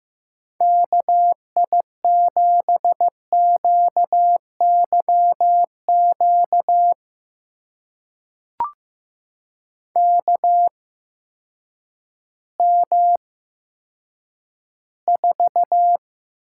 Morse code, heard as a keyed tone, about 15 wpm, spaced as for 9 wpm, 700 Hz. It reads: KI7QYQ KM4
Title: USA Call Signs - 6 Characters - 8x Character Spacing - 15wpm